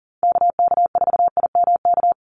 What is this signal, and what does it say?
Morse code, heard as a keyed tone, about 40 wpm, 700 Hz. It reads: KK4IMK
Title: USA Call Signs 40wpm